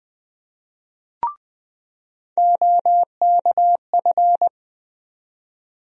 Morse code, heard as a keyed tone, about 21 wpm, 700 Hz. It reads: OKF
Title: USA Call Sign Postfixes - 3 Letters 20wpm